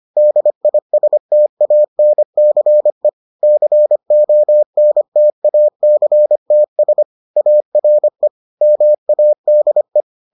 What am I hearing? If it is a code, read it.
Morse code, heard as a keyed tone, about 25 wpm, 600 Hz. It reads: DISTANCE CONTACTS ARE MADE